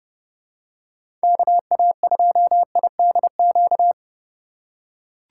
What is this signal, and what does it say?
Morse code, heard as a keyed tone, about 30 wpm, 700 Hz. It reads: KA2IDQ
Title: USA Call Signs - 6 Characters - 2x Character Spacing - 30wpm